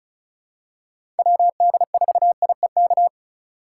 Morse code, heard as a keyed tone, about 36 wpm, 690 Hz. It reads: WD4IEK
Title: USA Call Signs - 6 Characters - 4x Character Spacing - 35wpm